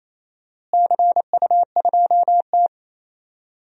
Morse code, heard as a keyed tone, about 28 wpm, 700 Hz. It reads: CU2T
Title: DXCC-100 Call Signs - 4 Characters - Repeated 3x - 28wpm